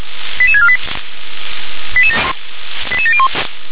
Five-tone selective call (ZVEI), heard as quick runs of five tones, tones of about 70 ms. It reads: 88648 88701